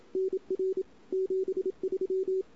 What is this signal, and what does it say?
Morse code, heard as a keyed tone, about 32 wpm, 410 Hz. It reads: NR 73
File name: CW / 2008-ss-cw-k5tr-op-n5rz -23:00